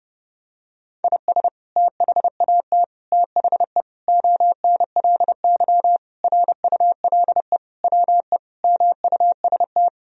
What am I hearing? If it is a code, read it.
Morse code, heard as a keyed tone, about 32 wpm, 700 Hz. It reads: IS THAT THE ONLY RULE WE MUST
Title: Sentences from Top 1000 Words - Review Part 2 - 30wpm